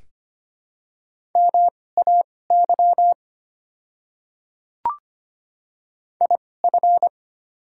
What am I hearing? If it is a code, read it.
Morse code, heard as a keyed tone, about 25 wpm, spaced as for 18 wpm, 700 Hz. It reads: MAY IF
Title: 2x Character Spacing - Top 200 Words - 25wpm